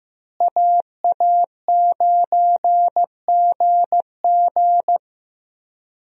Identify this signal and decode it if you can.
Morse code, heard as a keyed tone, about 15 wpm, 700 Hz. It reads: AA9GG
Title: USA Call Signs - 5 Characters - Repeated 2x - 15wpm